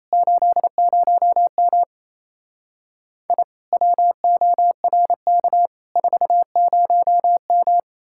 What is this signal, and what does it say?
Morse code, heard as a keyed tone, about 36 wpm, 700 Hz. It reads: 80M I WORK 40M